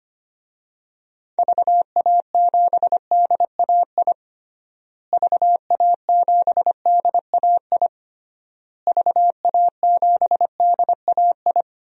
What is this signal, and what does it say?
Morse code, heard as a keyed tone, about 25 wpm, 700 Hz. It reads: VA7DAI VA7DAI VA7DAI
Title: DXCC-100 Call Signs - 6 Characters - Repeated 3x - 25wpm